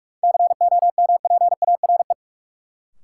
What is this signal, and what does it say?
Morse code, heard as a keyed tone, about 46 wpm, 700 Hz. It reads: COMPARE
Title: Sentences from Top 1300 Words - New Word Drill - 45wpm